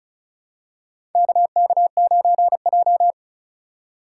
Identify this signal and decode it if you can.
Morse code, heard as a keyed tone, about 36 wpm, 700 Hz. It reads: KK9J
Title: USA Call Signs - 4 Characters - 2x Character Spacing - 35wpm